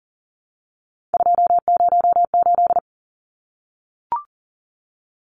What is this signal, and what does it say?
Morse code, heard as a keyed tone, about 40 wpm, 700 Hz. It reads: 208
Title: Numbers - 3-Digit 40wpm